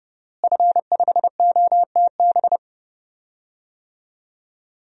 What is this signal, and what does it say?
Morse code, heard as a keyed tone, about 30 wpm, 700 Hz. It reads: F5OTB
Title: DXCC-100 Call Signs - 5 Characters - Repeated 3x - 30wpm